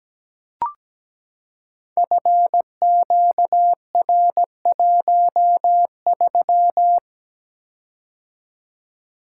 Morse code, heard as a keyed tone, about 17 wpm, 700 Hz. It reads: FQR13
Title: Code Groups - 5 Letter:Numbers - 17wpm